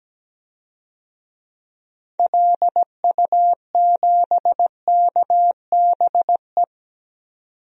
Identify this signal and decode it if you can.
Morse code, heard as a keyed tone, about 17 wpm, 700 Hz. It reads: LU7KBE
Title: DXCC-100 Call Signs - 6 Characters - 17wpm